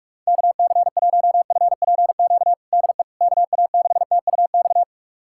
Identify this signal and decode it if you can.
Morse code, heard as a keyed tone, about 46 wpm, 690 Hz. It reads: KC1FPQ DE KA6TUX
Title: USA Callsign DE Callsign 45wpm